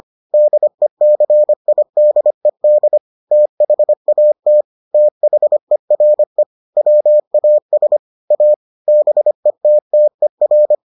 Morse code, heard as a keyed tone, about 25 wpm, 600 Hz. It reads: DECIDED THAT THERE WAS A BETTER